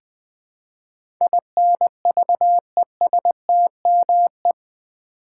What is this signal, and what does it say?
Morse code, heard as a keyed tone, about 20 wpm, 700 Hz. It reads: INVESTME
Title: Build A Word - 4 Letters At A Time - 10-Letter Words - 20wpm